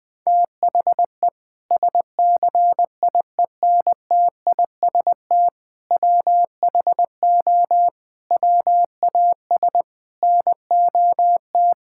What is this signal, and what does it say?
Morse code, heard as a keyed tone, about 21 wpm, 700 Hz. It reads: THE SCIENTIST WHO WAS NOT
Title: Sentences with 8-Word Phrases - 20wpm